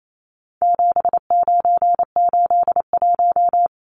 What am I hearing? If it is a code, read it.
Morse code, heard as a keyed tone, about 28 wpm, 700 Hz. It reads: 7981
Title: Numbers - 4-Digit 28wpm